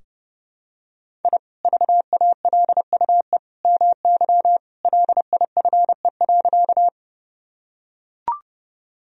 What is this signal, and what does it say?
Morse code, heard as a keyed tone, about 32 wpm, 700 Hz. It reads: I VALUE MY LIFE.
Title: Sentences from Top 1000 Words - Review Part 3 - 30wpm